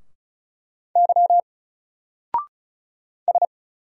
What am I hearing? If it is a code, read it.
Morse code, heard as a keyed tone, about 35 wpm, 700 Hz. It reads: Y S
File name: Single-Letter-Number-35wpm